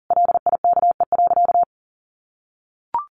Morse code, heard as a keyed tone, about 40 wpm, 700 Hz. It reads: LIKE.
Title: Sentences from Top 1200 Words 40wpm